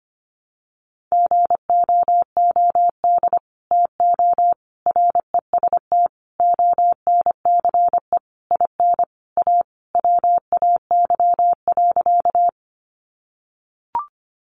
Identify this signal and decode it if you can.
Morse code, heard as a keyed tone, about 25 wpm, 700 Hz. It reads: GOOD TO REST ONCE IN A WAY.
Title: Sentences from Top 500 Words 25wpm